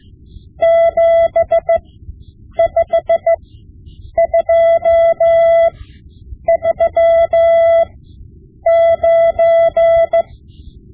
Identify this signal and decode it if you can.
Morse code, heard as a keyed tone, about 15 wpm, spaced as for 8 wpm, 670 Hz. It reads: 75239
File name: Morse Lessons / Morse Lessons / Tape-5B